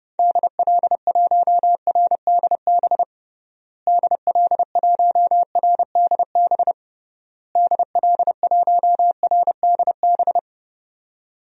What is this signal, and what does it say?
Morse code, heard as a keyed tone, about 30 wpm, 700 Hz. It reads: DL1RDB DL1RDB DL1RDB
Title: DXCC-100 Call Signs - 6 Characters - Repeated 3x - 30wpm